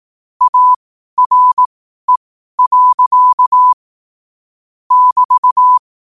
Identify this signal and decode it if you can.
Morse code, heard as a keyed tone, about 18 wpm, spaced as for 12 wpm, 1000 Hz. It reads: ARE. <BT>
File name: qotdfiles / 12wpm